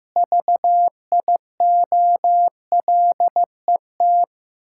Morse code, heard as a keyed tone, about 16 wpm, 700 Hz. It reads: VIOLET
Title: Pangram Challenge v2 - 15wpm